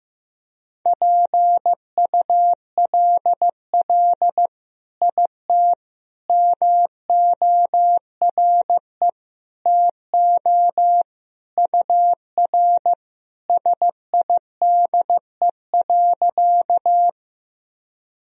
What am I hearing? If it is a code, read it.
Morse code, heard as a keyed tone, about 15 wpm, 700 Hz. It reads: PULL IT MORE TO UR SIDE.
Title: Sentences from Top 400 Words and Abbreviations 15wpm